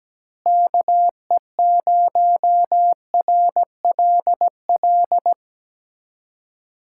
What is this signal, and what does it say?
Morse code, heard as a keyed tone, about 17 wpm, 700 Hz. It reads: KE0RLL
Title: USA Callsign DE Callsign - 6 Characters - 17wpm